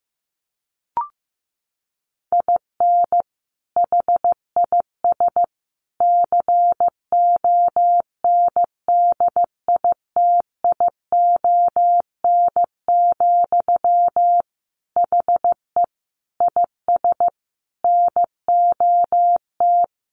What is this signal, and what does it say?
Morse code, heard as a keyed tone, about 15 wpm, 700 Hz. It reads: IN HIS CONDITION, HE IS NOT
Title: Sentences from Top 1200 Words 15wpm